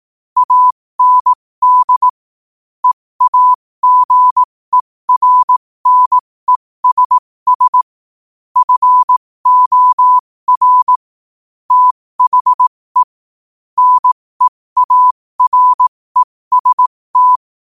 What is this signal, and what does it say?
Morse code, heard as a keyed tone, about 18 wpm, spaced as for 15 wpm, 1000 Hz. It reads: AND EAGERNESS FOR THE NEAREST
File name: qotdfiles / 15wpm